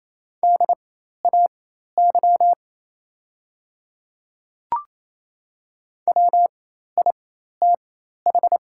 Morse code, heard as a keyed tone, about 28 wpm, spaced as for 13 wpm, 700 Hz. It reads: DAY WITH
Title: 4x Character Spacing - Top 100 Words - 28wpm